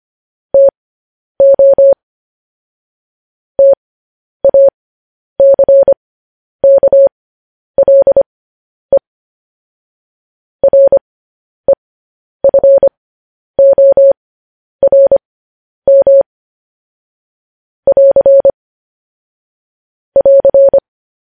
Morse code, heard as a keyed tone, about 25 wpm, spaced as for 10 wpm, 550 Hz. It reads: TO TACKLE REFORM <AR> <AR>
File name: podcast / News headlines-1x-25x10